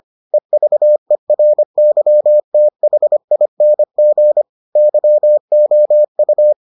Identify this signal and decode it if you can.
Morse code, heard as a keyed tone, about 25 wpm, 600 Hz. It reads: EVERYTHING YOU